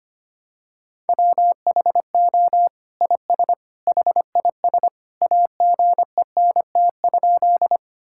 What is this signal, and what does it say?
Morse code, heard as a keyed tone, about 25 wpm, 700 Hz. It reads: WHO IS HIS AGENT?
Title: Sentences from Top 1600 Words 25wpm